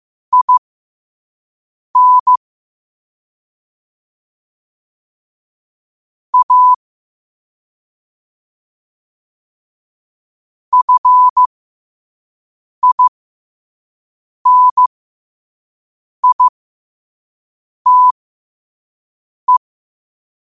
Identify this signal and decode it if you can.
Morse code, heard as a keyed tone, about 15 wpm, spaced as for 5 wpm, 1000 Hz. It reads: IN A FINITE